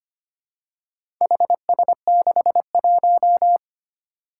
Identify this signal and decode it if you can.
Morse code, heard as a keyed tone, about 25 wpm, 700 Hz. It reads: HS61
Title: Four Challenging Letter-Number Combinations 25wpm